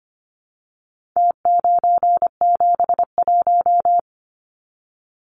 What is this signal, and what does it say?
Morse code, heard as a keyed tone, about 25 wpm, 700 Hz. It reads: T971
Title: Numbers - Contest Style - 4-Digit 25wpm